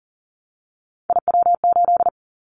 Morse code, heard as a keyed tone, about 39 wpm, 700 Hz. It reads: IW8
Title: Callsign Prefixes for Top 50 DXCC - 40wpm